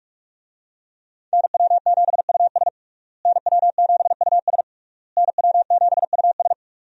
Morse code, heard as a keyed tone, about 41 wpm, 700 Hz. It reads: NW7US NW7US NW7US
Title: USA Call Signs - 5 Characters - Repeated 3x - 45wpm